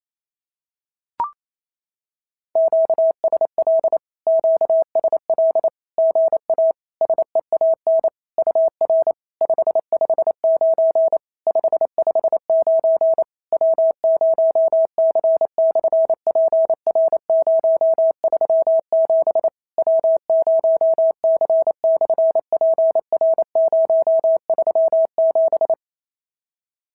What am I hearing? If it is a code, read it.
Morse code, heard as a keyed tone, about 28 wpm, 650 Hz. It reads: QSL QSL GA SEAN UR 559 559 W0C/PR037 W0C/PR037
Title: SOTA QSO Practice 28wpm